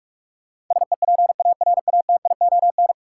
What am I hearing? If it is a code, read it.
Morse code, heard as a keyed tone, about 46 wpm, 700 Hz. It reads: SEPARATION
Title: Sets of 5 Words - New Word Drill - 45wpm